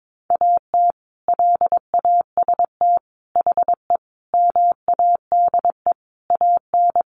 Morse code, heard as a keyed tone, about 22 wpm, 700 Hz. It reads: AT LAST HE MADE AN
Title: Sentences from Top 1200 Words 22wpm